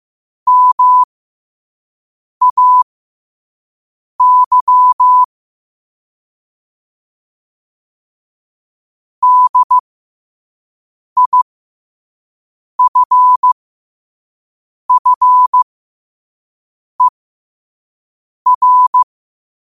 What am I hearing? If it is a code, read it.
Morse code, heard as a keyed tone, about 15 wpm, spaced as for 5 wpm, 1000 Hz. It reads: MAY DIFFER
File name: qotdfiles / 5wpm